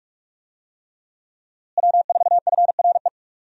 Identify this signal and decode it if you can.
Morse code, heard as a keyed tone, about 46 wpm, 700 Hz. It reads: W4FRE
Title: USA Call Signs - 5 Characters - Repeated 3x - 45wpm